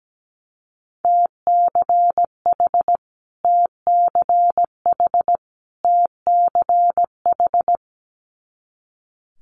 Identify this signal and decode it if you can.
Morse code, heard as a keyed tone, about 17 wpm, 700 Hz. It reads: TCH TCH TCH
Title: Multi-letter Phonemes Repeated 3x 17wpm